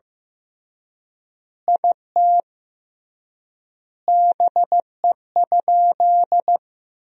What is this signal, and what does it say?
Morse code, heard as a keyed tone, about 15 wpm, 700 Hz. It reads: IT BE?
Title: 3x Word-Spacing - Sentences from Top 100 Words - 15wpm